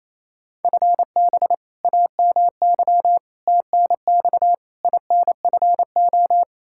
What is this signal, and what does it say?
Morse code, heard as a keyed tone, about 28 wpm, 700 Hz. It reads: FB AMY TNX INFO
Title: Practice QSO - Exchange 3 - More Info - 28wpm